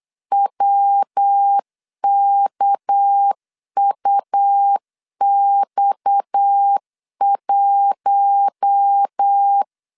Morse code, heard as a keyed tone, about 8 wpm, 800 Hz. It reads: WKUX1
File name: CW / DET / 1 6 1500C S C 8wpm 5219